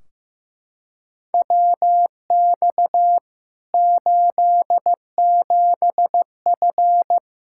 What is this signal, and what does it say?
Morse code, heard as a keyed tone, about 15 wpm, 700 Hz. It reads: WX 87F
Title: K1USN SST QSO Practice 15wpm